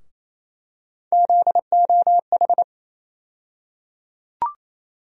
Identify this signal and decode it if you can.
Morse code, heard as a keyed tone, about 28 wpm, 700 Hz. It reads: ZOH
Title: USA Call Sign Postfixes - 3 Letters 28wpm